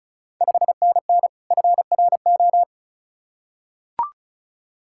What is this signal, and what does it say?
Morse code, heard as a keyed tone, about 36 wpm, 690 Hz. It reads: 5NN FRO
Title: Ontario QSO Party Exchange 35wpm